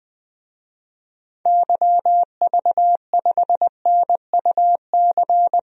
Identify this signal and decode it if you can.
Morse code, heard as a keyed tone, about 20 wpm, 700 Hz. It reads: YV5NUC
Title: DXCC-100 Call Signs - 6 Characters - Repeated 2x - 20wpm